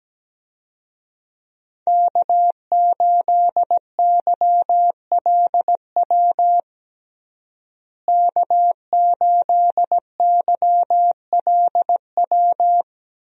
Morse code, heard as a keyed tone, about 17 wpm, 700 Hz. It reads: K8YLW K8YLW
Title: USA Call Signs - 5 Characters - Repeated 3x - 17wpm